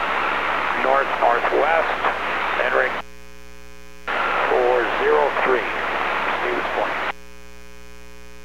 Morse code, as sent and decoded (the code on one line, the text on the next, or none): none